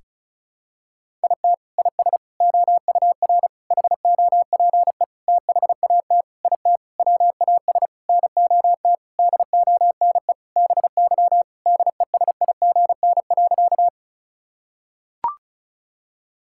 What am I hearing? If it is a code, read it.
Morse code, heard as a keyed tone, about 36 wpm, 700 Hz. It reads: IT IS OUR HOPE THAT IT WAS NOT DONE BY DESIGN.
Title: Sentences from Top 1000 Words - Review Part 5 - 35wpm